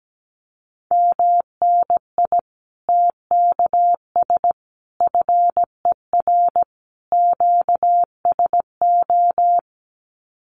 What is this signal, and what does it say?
Morse code, heard as a keyed tone, about 17 wpm, 700 Hz. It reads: MNI TKS FER QSO
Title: Common QSO Phrases 17wpm